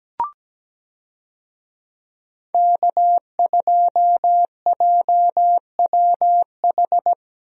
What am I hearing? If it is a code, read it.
Morse code, heard as a keyed tone, about 17 wpm, 700 Hz. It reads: K2JWH
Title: USA Call Signs - 5 Characters - Repeated 2x - 17wpm